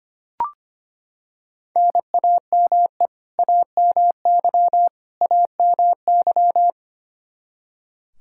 Morse code, heard as a keyed tone, about 25 wpm, 700 Hz. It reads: NAME AMY AMY
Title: Practice QSO - Exchange 1 - Beginning - 25wpm